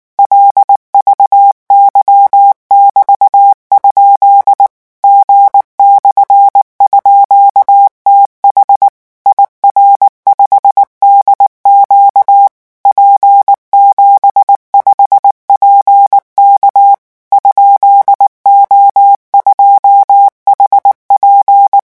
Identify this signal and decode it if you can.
Morse code, heard as a keyed tone, about 19 wpm, 800 Hz. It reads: LVY<BT>? G/_TH IR5DQ P75PK ?O2HP
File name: morse mp3 / revision13-vitesse 18 mots